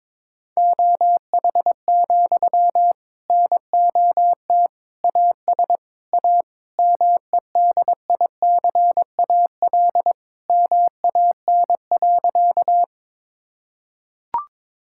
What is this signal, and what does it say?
Morse code, heard as a keyed tone, about 22 wpm, 700 Hz. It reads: OH, NOT AS A MEDICAL MAN.
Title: Sentences from Top 1000 Words - Review Part 5 - 22wpm